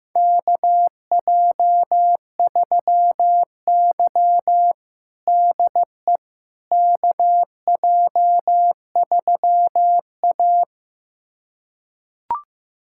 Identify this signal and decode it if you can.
Morse code, heard as a keyed tone, about 15 wpm, 700 Hz. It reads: KJ3Y DE KJ3A
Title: USA Callsign DE Callsign - 4 Characters - 15wpm